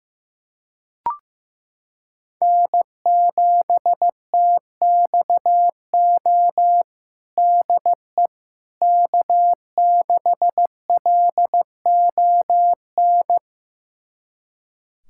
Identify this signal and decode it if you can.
Morse code, heard as a keyed tone, about 15 wpm, 700 Hz. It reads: N7TXO DE K6LON
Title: USA Callsign DE Callsign - 5 Characters - 15wpm